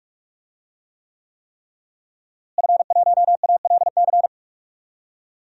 Morse code, heard as a keyed tone, about 46 wpm, 700 Hz. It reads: F1ALC
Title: DXCC-100 Call Signs - 5 Characters - Repeated 3x - 45wpm